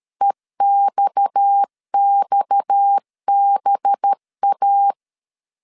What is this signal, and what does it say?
Morse code, heard as a keyed tone, about 13 wpm, 800 Hz. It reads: EXXBA
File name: LEZIONI / 6 PKX0 / 6 PKX 12wpm 2024